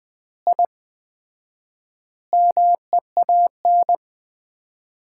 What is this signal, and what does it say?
Morse code, heard as a keyed tone, about 20 wpm, 700 Hz. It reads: I MEAN
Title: 4x Word-Spacing - Sets of 5 Words - 20wpm